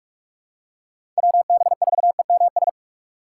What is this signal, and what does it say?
Morse code, heard as a keyed tone, about 44 wpm, 700 Hz. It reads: WB4EMS
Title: USA Call Signs - 6 Characters - 3x Character Spacing - 45wpm